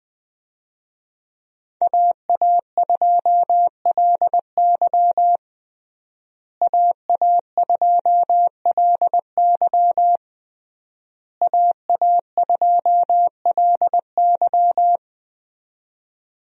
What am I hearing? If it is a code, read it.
Morse code, heard as a keyed tone, about 21 wpm, 700 Hz. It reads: AA2LY AA2LY AA2LY
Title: USA Call Signs - 5 Characters - Repeated 3x - 20wpm